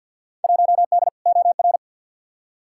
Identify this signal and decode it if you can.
Morse code, heard as a keyed tone, about 51 wpm, 700 Hz. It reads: 1D OR
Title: Field Day Exchanges - 50wpm